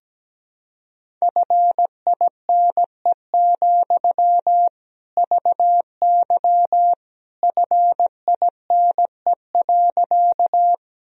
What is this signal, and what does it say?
Morse code, heard as a keyed tone, about 17 wpm, 700 Hz. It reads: FINE, VY FINE.
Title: Sentences from Top 500 Words and Abbreviations 17wpm